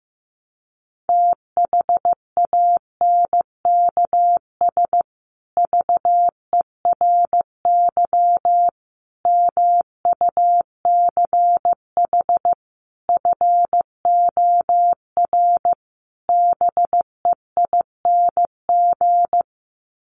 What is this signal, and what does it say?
Morse code, heard as a keyed tone, about 15 wpm, 700 Hz. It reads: THANKS VERY MUCH FOR BEING
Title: Sets of 5 Words Encore 15wpm